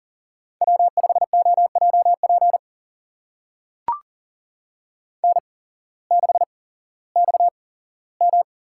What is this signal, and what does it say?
Morse code, heard as a keyed tone, about 40 wpm, spaced as for 19 wpm, 700 Hz. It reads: W5OJP N6XM
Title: USA Call Signs - 5 Characters - 8x Character Spacing - 40wpm